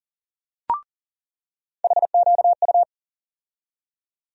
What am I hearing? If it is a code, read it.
Morse code, heard as a keyed tone, about 40 wpm, 700 Hz. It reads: HQU
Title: USA Call Sign Postfixes - 3 Letters 40wpm